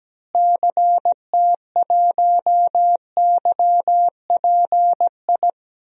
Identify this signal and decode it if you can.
Morse code, heard as a keyed tone, about 17 wpm, 700 Hz. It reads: CT1YPI
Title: DXCC-100 Call Signs - 6 Characters - Repeated 3x - 17wpm